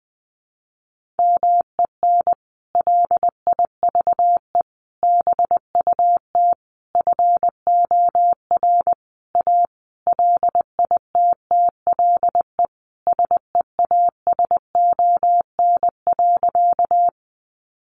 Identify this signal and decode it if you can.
Morse code, heard as a keyed tone, about 20 wpm, 700 Hz. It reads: MEN LIVE BUT FOR A LITTLE SEASON.
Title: Sentences from Top 1400 Words 20wpm